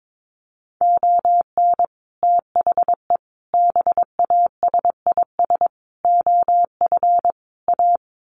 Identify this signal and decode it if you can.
Morse code, heard as a keyed tone, about 22 wpm, 700 Hz. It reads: ON THE BASIS OF A
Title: Sets of 5 Words Encore 22wpm